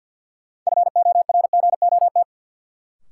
Morse code, heard as a keyed tone, about 46 wpm, 700 Hz. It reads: FORGOT
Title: Sentences from Top 1400 Words - New Word Drill - 50wpm